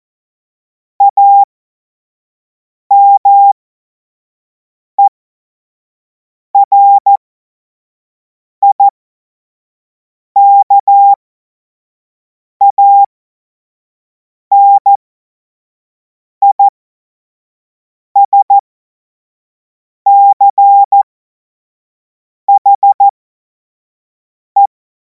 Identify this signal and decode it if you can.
Morse code, heard as a keyed tone, about 14 wpm, spaced as for 5 wpm, 800 Hz. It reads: AMERIKANISCHE
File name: Morsekurs / Witze / Witz 24. Februar 2026 25 F70